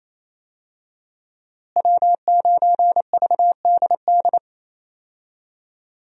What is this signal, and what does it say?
Morse code, heard as a keyed tone, about 28 wpm, 700 Hz. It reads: W9VDD
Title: USA Call Signs - 5 Characters - Repeated 2x - 28wpm